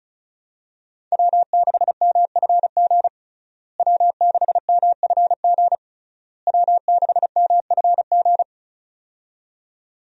Morse code, heard as a keyed tone, about 36 wpm, 700 Hz. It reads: W6MFG W6MFG W6MFG
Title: USA Call Signs - 5 Characters - Repeated 3x - 35wpm